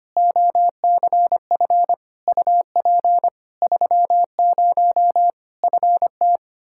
Morse code, heard as a keyed tone, about 25 wpm, 700 Hz. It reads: OCF UP 30 FT
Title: Practice QSO - Exchange 2 - Info - 25wpm